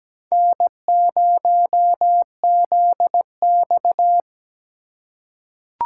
Morse code, heard as a keyed tone, about 17 wpm, 700 Hz. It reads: N0ZX
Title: USA Call Signs - 4 Characters - 3x Character Spacing - 17wpm